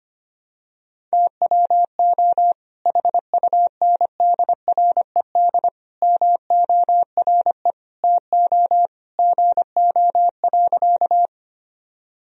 Morse code, heard as a keyed tone, about 25 wpm, 700 Hz. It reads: TWO HUNDRED MORE TO GO.
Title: Sentences from Top 1000 Words - Review Part 2 - 25wpm